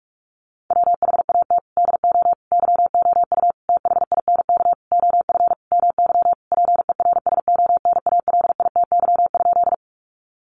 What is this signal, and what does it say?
Morse code, heard as a keyed tone, about 44 wpm, 700 Hz. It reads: WHAT DO YOU THINK OF MY PERSONALITY?
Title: Sentences from Top 1400 Words 45wpm